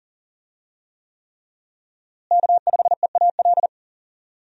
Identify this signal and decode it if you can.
Morse code, heard as a keyed tone, about 38 wpm, 700 Hz. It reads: K5EAL
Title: USA Call Signs - 5 Characters - Repeated 3x - 40wpm